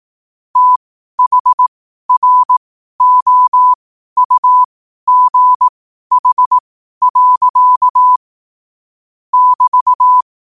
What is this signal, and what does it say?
Morse code, heard as a keyed tone, about 18 wpm, spaced as for 12 wpm, 1000 Hz. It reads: THROUGH. <BT>
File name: 12wpm